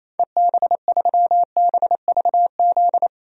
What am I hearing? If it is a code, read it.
Morse code, heard as a keyed tone, about 28 wpm, 700 Hz. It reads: EB3BVZ
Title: DXCC-100 Call Signs - 6 Characters - 28wpm